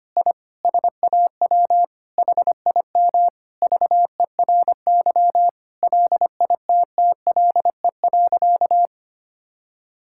Morse code, heard as a keyed tone, about 25 wpm, 700 Hz. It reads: I SAW HIM VERY LITTLE.
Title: Sentences from Top 1000 Words - Review Part 5 - 25wpm